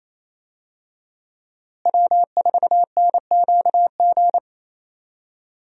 Morse code, heard as a keyed tone, about 28 wpm, 700 Hz. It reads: W4NQG
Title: USA Call Signs - 5 Characters - Repeated 2x - 28wpm